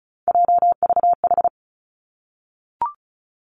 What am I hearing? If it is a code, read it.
Morse code, heard as a keyed tone, about 36 wpm, 700 Hz. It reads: JVH
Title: Three Challenging Letter-Number Combinations 35wpm